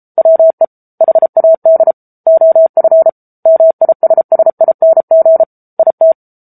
Morse code, heard as a keyed tone, about 34 wpm, 650 Hz. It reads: WE HAD OF MISSING IT
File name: Book Two / 07-Chapter 7 - 33 WPM